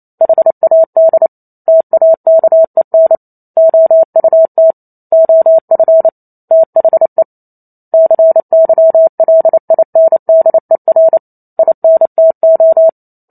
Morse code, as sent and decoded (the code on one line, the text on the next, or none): .... .- -.. / - .- -.- . -. / --- ..- - / --- ..-. / - .... / -.-. -.-- .-.. .. -. -.. .-. / .. -. - ---
HAD TAKEN OUT OF THE CYLINDER INTO